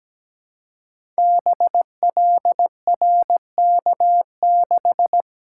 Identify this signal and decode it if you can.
Morse code, heard as a keyed tone, about 17 wpm, 700 Hz. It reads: BLRK6
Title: Code Groups - 5 Letter:Numbers - 17wpm